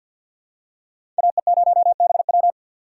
Morse code, heard as a keyed tone, about 53 wpm, 690 Hz. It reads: AE0BW
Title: USA Call Signs - 4x Character Spacing - 50wpm